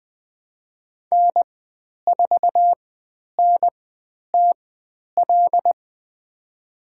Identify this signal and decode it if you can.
Morse code, heard as a keyed tone, about 20 wpm, spaced as for 10 wpm, 700 Hz. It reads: N4NTL
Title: 102 Lesson 17g - Letter u - Call Signs 20wpm